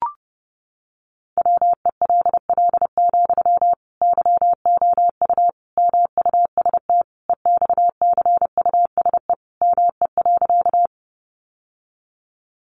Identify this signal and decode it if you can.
Morse code, heard as a keyed tone, about 30 wpm, 700 Hz. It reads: WELL, YOU MUST EXCUSE ME.
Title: Sentences from Top 700 Words 30wpm